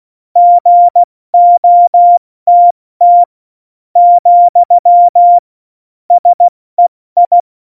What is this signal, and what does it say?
Morse code, heard as a keyed tone, about 16 wpm, spaced as for 14 wpm, 700 Hz. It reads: GOTT , SEI
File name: Morsekurs / Witze / Witz 09. Oktober 2018, 70 F80